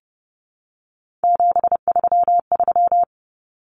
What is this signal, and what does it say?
Morse code, heard as a keyed tone, about 30 wpm, 700 Hz. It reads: 733
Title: Common Cut Numbers - 3-Digit 30wpm